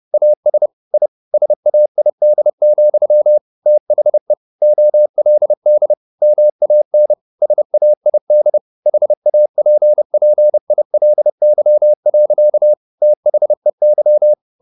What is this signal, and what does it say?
Morse code, heard as a keyed tone, about 30 wpm, 600 Hz. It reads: AS I SAID, THE OLD MAN SAID HAPPILY. THEY
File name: the old man and the sea / the old man and the sea 1 30wpm